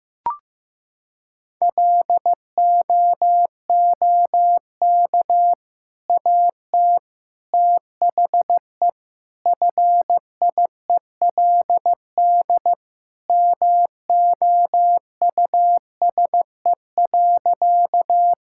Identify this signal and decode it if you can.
Morse code, heard as a keyed tone, about 16 wpm, 700 Hz. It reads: LOOK AT THE FIELD MOUSE.
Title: Sentences from Top 1000 Words - Review Part 5 - 15wpm